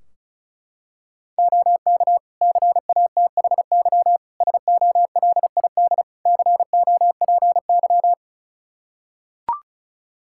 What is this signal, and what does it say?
Morse code, heard as a keyed tone, about 34 wpm, 700 Hz. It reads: OK CATHY SOLID COPY
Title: K1USN SST QSO Practice 35wpm